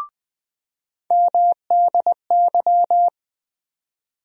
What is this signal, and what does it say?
Morse code, heard as a keyed tone, about 20 wpm, 700 Hz. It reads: MDY
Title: USA Call Sign Postfixes - 3 Letters 20wpm